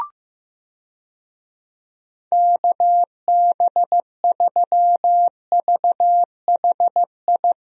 Morse code, heard as a keyed tone, about 15 wpm, 700 Hz. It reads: KB3VHI
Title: USA Call Signs - 6 Characters - Repeated 2x - 15wpm